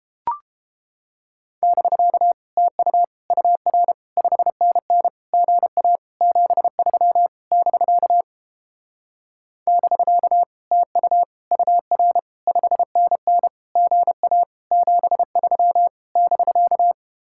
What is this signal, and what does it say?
Morse code, heard as a keyed tone, about 31 wpm, 700 Hz. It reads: <BK> TU UR 5NN GA 73 <BK> <BK> TU UR 5NN GA 73 <BK>